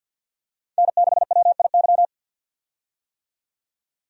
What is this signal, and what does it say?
Morse code, heard as a keyed tone, about 53 wpm, 700 Hz. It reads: N6WIY
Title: USA Call Signs - 5 Characters - 50wpm